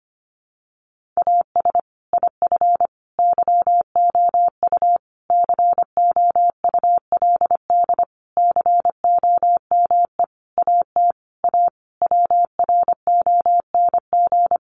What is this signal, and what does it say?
Morse code, heard as a keyed tone, about 25 wpm, 700 Hz. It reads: AS IF YOU COULD COME AT A WRONG
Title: Sentences from Top 900 Words 25wpm